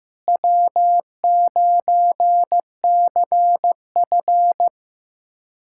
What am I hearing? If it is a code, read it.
Morse code, heard as a keyed tone, about 15 wpm, 700 Hz. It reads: W9CF
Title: USA Callsign DE Callsign - 4 Characters - 15wpm